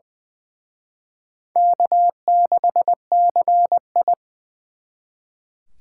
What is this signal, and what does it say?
Morse code, heard as a keyed tone, about 23 wpm, 700 Hz. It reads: K6CI
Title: Speed Racing - USA 1x2 Call Signs 20wpm